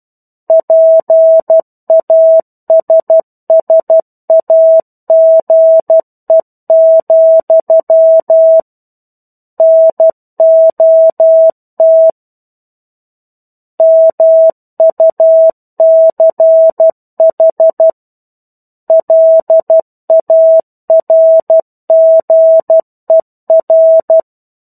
Morse code, heard as a keyed tone, about 12 wpm, 650 Hz. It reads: PASSAGE, NOT MUCH LARGER
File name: 01-Chapter-01 12 WPM